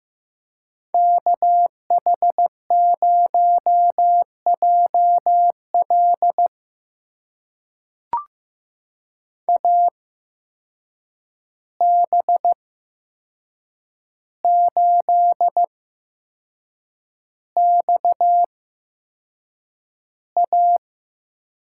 Morse code, heard as a keyed tone, about 15 wpm, spaced as for 6 wpm, 700 Hz. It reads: KH0JL AB8XA
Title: USA Call Signs - 8x Character Spacing - 15wpm